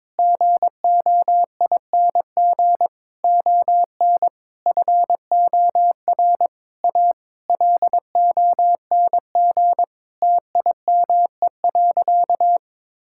Morse code, heard as a keyed tone, about 22 wpm, 700 Hz. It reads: GOING ON FOR A LONG TIME.
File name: practice / 22wpm / Sentences with 8-Word Phrases - 22wpm